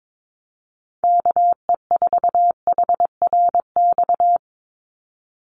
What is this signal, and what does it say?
Morse code, heard as a keyed tone, about 22 wpm, 700 Hz. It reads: KE4HRX
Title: USA Call Signs 22wpm